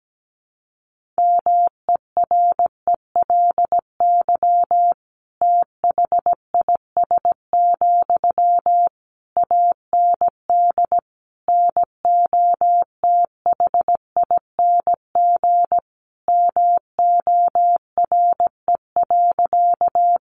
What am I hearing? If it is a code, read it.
Morse code, heard as a keyed tone, about 17 wpm, 700 Hz. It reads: MERELY THIS, AND NOTHING MORE.
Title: Sentences from Top 1400 Words 17wpm